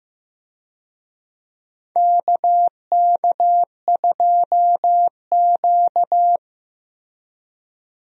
Morse code, heard as a keyed tone, about 15 wpm, 700 Hz. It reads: KK2Q
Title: USA Call Signs - 4 Characters - Repeated 3x - 15wpm